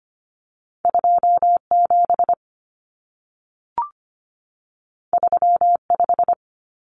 Morse code, heard as a keyed tone, about 25 wpm, 700 Hz. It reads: 27 35
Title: Common Cut Numbers - 2-Digit 25wpm